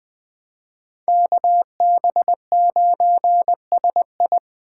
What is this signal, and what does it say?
Morse code, heard as a keyed tone, about 19 wpm, 700 Hz. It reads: KB9SI